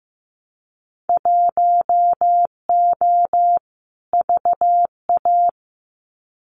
Morse code, heard as a keyed tone, about 15 wpm, 700 Hz. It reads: 1O VA
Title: Winter Field Day Exchanges 15wpm